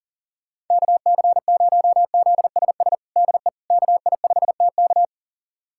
Morse code, heard as a keyed tone, about 40 wpm, 700 Hz. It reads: KC0ZSS DE KI5TK
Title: USA Callsign DE Callsign 40wpm